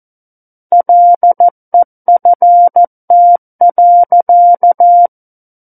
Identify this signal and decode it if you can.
Morse code, heard as a keyed tone, about 14 wpm, 700 Hz. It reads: LEFT.
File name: Book One / 05-Chapter 5 - 14 WPM